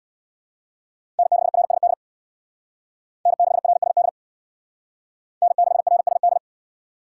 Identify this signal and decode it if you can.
Morse code, heard as a keyed tone, about 62 wpm, 700 Hz. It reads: N6RSD N6RSD N6RSD
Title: Speed Racing - USA 1x3 Call Signs 50wpm